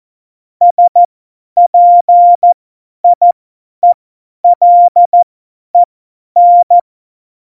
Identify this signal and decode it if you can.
Morse code, heard as a keyed tone, about 14 wpm, spaced as for 10 wpm, 700 Hz. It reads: SPIELEN
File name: Morsekurs / Witze / Witz 22. Dezember 2020, 50 F70